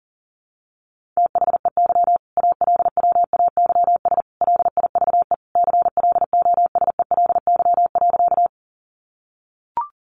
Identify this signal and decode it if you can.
Morse code, heard as a keyed tone, about 37 wpm, 700 Hz. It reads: THEY ALWAYS LIVE CLOSELY.
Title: Sentences from Top 1400 Words 40wpm